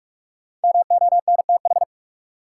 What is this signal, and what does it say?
Morse code, heard as a keyed tone, about 46 wpm, 700 Hz. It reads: MONTH
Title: Sentences from Top 800 Words - New Word Drill - 45wpm